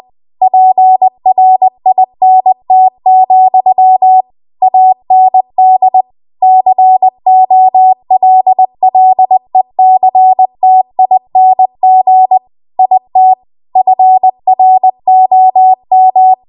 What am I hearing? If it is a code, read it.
Morse code, heard as a keyed tone, about 20 wpm, 750 Hz. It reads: PRINT, AND COLLECTING IT FROM
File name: Archive / 20 WPM / 20WPM